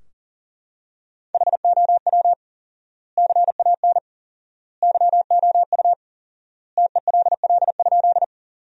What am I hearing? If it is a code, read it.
Morse code, heard as a keyed tone, about 40 wpm, 700 Hz. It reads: HOW CAN YOU TELL?
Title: 4x Word-Spacing - Sentences from Top 200 Words - 40wpm